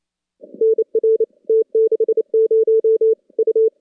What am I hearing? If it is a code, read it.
Morse code, heard as a keyed tone, about 28 wpm, 410 Hz. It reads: NR T60 U
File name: contests / CW-SS-k5tr- -2200